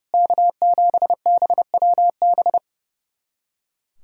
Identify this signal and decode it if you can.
Morse code, heard as a keyed tone, about 30 wpm, 700 Hz. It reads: K7BWB
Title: USA Callsign DE Callsign 30wpm